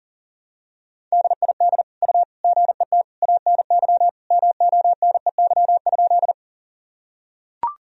Morse code, heard as a keyed tone, about 37 wpm, 700 Hz. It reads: DID U GET ANY MONEY?
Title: Sentences from Top 400 Words and Abbreviations 40wpm